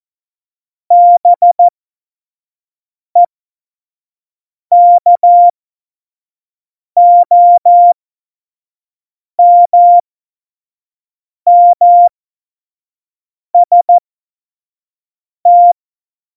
Morse code, heard as a keyed tone, about 14 wpm, spaced as for 5 wpm, 700 Hz. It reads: BEKOMMST
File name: Morsekurs / Witze / Witz 05. Juni 2018, 25 F70